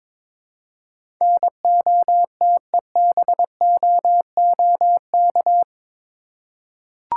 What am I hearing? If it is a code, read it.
Morse code, heard as a keyed tone, about 22 wpm, 700 Hz. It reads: NOTEBOOK
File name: practice / 22wpm / Eight Letter Words - 22wpm